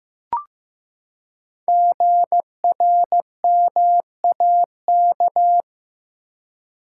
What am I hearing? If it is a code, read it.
Morse code, heard as a keyed tone, about 15 wpm, 700 Hz. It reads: GRMAK
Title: Code Groups - 5 Letters - 15wpm